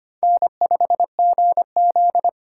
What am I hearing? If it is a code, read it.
Morse code, heard as a keyed tone, about 32 wpm, 700 Hz. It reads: N5GZ